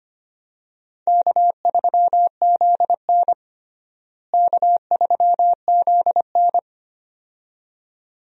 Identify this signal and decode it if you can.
Morse code, heard as a keyed tone, about 25 wpm, 700 Hz. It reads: K3ZN K3ZN
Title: USA Call Signs - 4 Characters - Repeated 2x - 25wpm